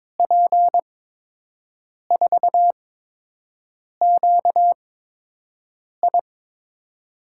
Morse code, heard as a keyed tone, about 23 wpm, spaced as for 6 wpm, 700 Hz. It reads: P4QI
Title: USA Call Signs - 5 Characters - 8x Character Spacing - 22wpm